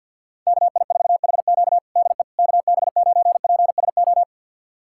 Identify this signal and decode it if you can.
Morse code, heard as a keyed tone, about 48 wpm, 700 Hz. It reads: KI4HQ DE KB9PSO
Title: USA Callsign DE Callsign 50wpm